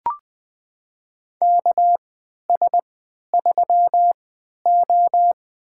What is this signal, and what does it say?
Morse code, heard as a keyed tone, about 21 wpm, spaced as for 12 wpm, 700 Hz. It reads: KS3O
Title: USA Call Signs - 4 Characters - 3x Character Spacing - 20wpm